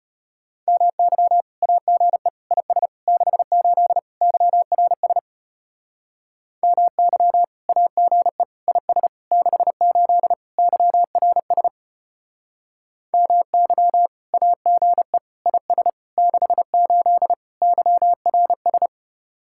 Morse code, heard as a keyed tone, about 30 wpm, 700 Hz. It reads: MY AGE IS 68 YRS MY AGE IS 68 YRS MY AGE IS 68 YRS
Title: Speed-Racing Common QSO Phrases Encore - 25wpm